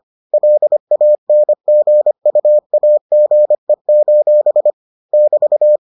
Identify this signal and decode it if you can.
Morse code, heard as a keyed tone, about 25 wpm, 600 Hz. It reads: LANGUAGE: <BT>